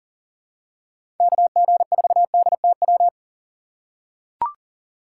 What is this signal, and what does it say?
Morse code, heard as a keyed tone, about 38 wpm, 700 Hz. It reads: KG4DTW
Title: USA Call Signs - 6 Characters - 6x Character Spacing - 40wpm